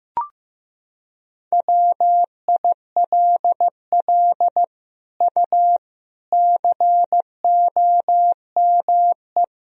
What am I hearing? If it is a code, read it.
Morse code, heard as a keyed tone, about 15 wpm, 700 Hz. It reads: WILL U COME
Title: Sentences from Top 200 Words and Abbreviations 15wpm